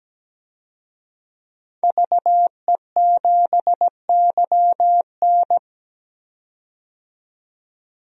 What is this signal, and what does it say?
Morse code, heard as a keyed tone, about 17 wpm, 700 Hz. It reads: VE7YN
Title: DXCC-100 Call Signs - 5 Characters - 17wpm